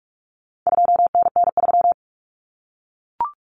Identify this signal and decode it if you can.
Morse code, heard as a keyed tone, about 44 wpm, 700 Hz. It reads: 2NN3
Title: Common Cut Numbers - 4-Digit 45wpm